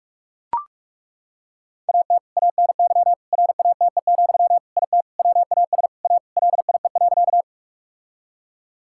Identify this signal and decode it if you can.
Morse code, heard as a keyed tone, about 41 wpm, 700 Hz. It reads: AT ANY RATE, IT WAS A LIE.
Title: Sentences from Top 1600 Words 45wpm